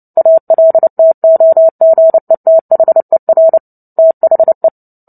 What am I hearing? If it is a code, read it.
Morse code, heard as a keyed tone, about 30 wpm, 650 Hz. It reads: ALTOGETHER THE
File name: Book Two / 03-Chapter 3 - 29 WPM